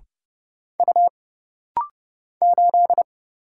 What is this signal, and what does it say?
Morse code, heard as a keyed tone, about 30 wpm, 700 Hz. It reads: U 8
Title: Single Letter-Number - Rapid-Fire 30wpm